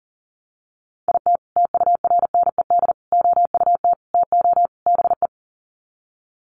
Morse code, heard as a keyed tone, about 37 wpm, 700 Hz. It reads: IT TURNED OUT TO BE
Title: Sets of 5 Words 40wpm